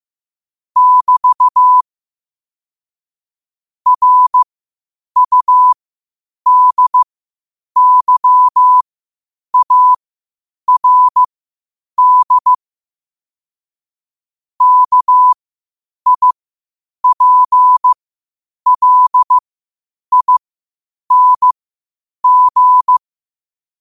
Morse code, heard as a keyed tone, about 15 wpm, spaced as for 8 wpm, 1000 Hz. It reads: <BT> RUDYARD KIPLING
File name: qotdfiles / 8wpm